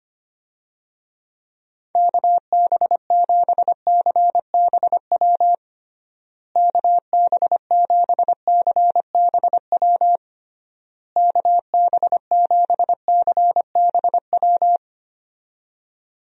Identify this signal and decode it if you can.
Morse code, heard as a keyed tone, about 25 wpm, 700 Hz. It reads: KB7CBW KB7CBW KB7CBW
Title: USA Call Signs - 6 Characters - Repeated 3x - 25wpm